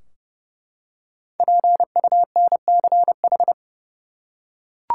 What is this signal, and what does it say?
Morse code, heard as a keyed tone, about 29 wpm, 700 Hz. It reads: PUNCH
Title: Five Letter Words - 30wpm